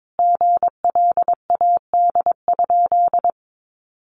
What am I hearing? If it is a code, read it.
Morse code, heard as a keyed tone, about 22 wpm, 700 Hz. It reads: GLAD?
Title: Sentences from Top 800 Words 22wpm